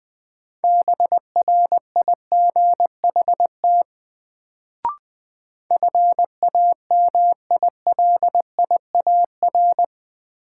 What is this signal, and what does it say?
Morse code, heard as a keyed tone, about 20 wpm, 700 Hz. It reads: BRIGHT FAMILIAR
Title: Sentences from Top 1000 Words - New Word Drill - 20wpm